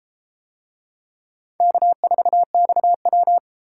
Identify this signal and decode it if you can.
Morse code, heard as a keyed tone, about 33 wpm, 700 Hz. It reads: K4XW